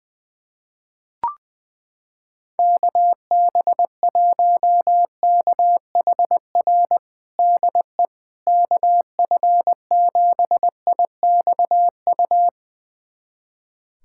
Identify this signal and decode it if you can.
Morse code, heard as a keyed tone, about 21 wpm, 700 Hz. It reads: KB1KHR DE KF7IXU
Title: USA Callsign DE Callsign - 6 Characters - 20wpm